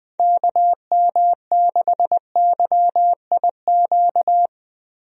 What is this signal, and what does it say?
Morse code, heard as a keyed tone, about 21 wpm, 700 Hz. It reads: KM6YIQ
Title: USA Call Signs - 6 Characters - Repeated 2x - 20wpm